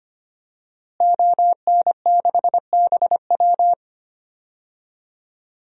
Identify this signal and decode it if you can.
Morse code, heard as a keyed tone, about 25 wpm, 700 Hz. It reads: ON6BW
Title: DXCC-100 Call Signs - 5 Characters - Repeated 2x - 25wpm